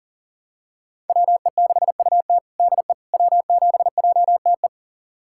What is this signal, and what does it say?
Morse code, heard as a keyed tone, about 40 wpm, 700 Hz. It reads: WE6UT DE W7JTE
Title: USA Callsign DE Callsign - 5 Characters - 40wpm